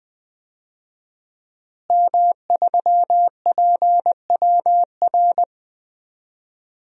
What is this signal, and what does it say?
Morse code, heard as a keyed tone, about 20 wpm, 700 Hz. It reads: M3PWR
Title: DXCC-100 Call Signs - 5 Characters - Repeated 2x - 20wpm